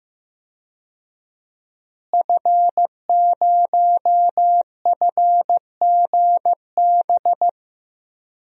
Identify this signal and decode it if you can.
Morse code, heard as a keyed tone, about 15 wpm, 700 Hz. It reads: F0FGB
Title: DXCC-100 Call Signs - 5 Characters - Repeated 3x - 15wpm